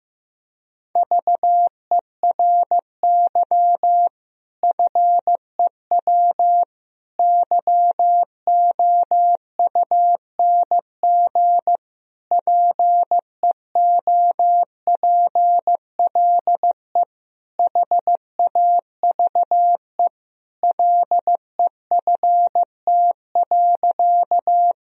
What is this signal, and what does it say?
Morse code, heard as a keyed tone, about 15 wpm, 700 Hz. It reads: VERY FEW YOUNG PEOPLE HAVE LEFT.
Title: Sentences from Top 1000 Words - Review Part 1 - 15wpm